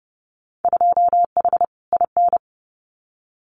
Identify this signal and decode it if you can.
Morse code, heard as a keyed tone, about 30 wpm, 700 Hz. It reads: 2H IN
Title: Winter Field Day Exchanges Encore 30wpm